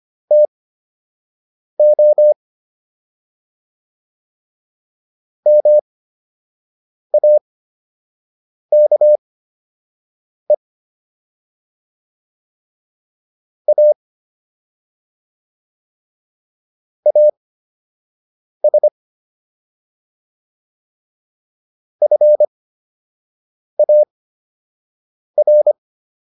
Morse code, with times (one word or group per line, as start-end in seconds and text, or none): 0.3-2.3 TO
5.5-10.5 MAKE
13.7-13.9 A
17.1-18.9 AS
22.0-25.7 FAR